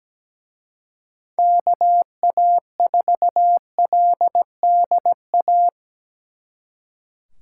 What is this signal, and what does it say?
Morse code, heard as a keyed tone, about 20 wpm, 700 Hz. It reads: KA4LDA
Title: Speed Racing - USA 2x3 Call Signs 17wpm